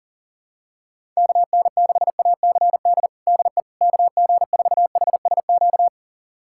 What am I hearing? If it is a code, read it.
Morse code, heard as a keyed tone, about 40 wpm, 700 Hz. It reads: KN6ACD DE KG4HSQ
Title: USA Callsign DE Callsign 40wpm